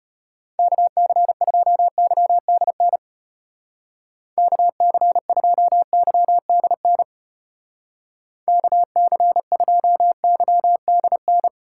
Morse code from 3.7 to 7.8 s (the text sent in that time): KC2YDN